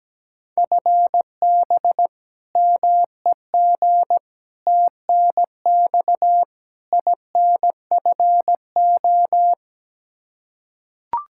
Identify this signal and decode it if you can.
Morse code, heard as a keyed tone, about 17 wpm, 700 Hz. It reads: FB MEG TNX INFO
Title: Practice QSO - Exchange 3 - More Info - 17wpm